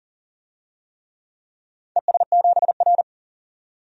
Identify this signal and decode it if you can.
Morse code, heard as a keyed tone, about 38 wpm, 700 Hz. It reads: ES7R
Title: DXCC-100 Call Signs - 4 Characters - 40wpm